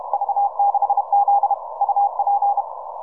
Morse code, heard as a keyed tone, about 30 wpm, 810 Hz. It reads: V6Z UP